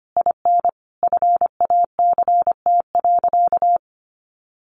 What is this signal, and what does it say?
Morse code, heard as a keyed tone, about 25 wpm, 700 Hz. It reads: IN FACT.
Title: Sentences from Top 600 Words 25wpm